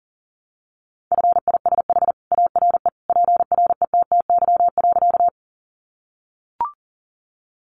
Morse code, heard as a keyed tone, about 40 wpm, 700 Hz. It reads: FISH ARE PRETTY.
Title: Additional Easy Phrases 40wpm